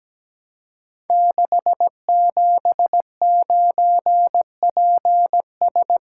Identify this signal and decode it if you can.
Morse code, heard as a keyed tone, about 17 wpm, 700 Hz. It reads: 679PS
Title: Code Groups - 5 Letter:Numbers - 17wpm